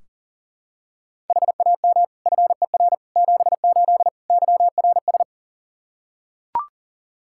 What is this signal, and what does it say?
Morse code, heard as a keyed tone, about 40 wpm, 700 Hz. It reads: HAM FER 78 YRS
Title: Practice QSO - Exchange 3 - More Info - 40wpm